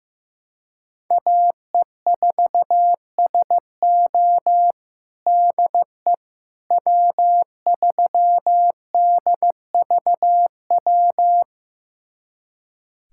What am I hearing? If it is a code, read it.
Morse code, heard as a keyed tone, about 15 wpm, 700 Hz. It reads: AE4SO DE W3DVW
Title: USA Callsign DE Callsign - 5 Characters - 15wpm